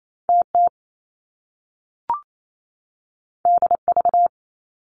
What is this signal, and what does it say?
Morse code, heard as a keyed tone, about 28 wpm, 700 Hz. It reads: TT DV
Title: Top Two Letter Combinations in Words 28wpm